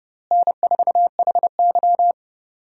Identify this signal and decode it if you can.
Morse code, heard as a keyed tone, about 30 wpm, 700 Hz. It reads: N4HY
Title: USA Call Signs - 4 Characters - Repeated 2x - 30wpm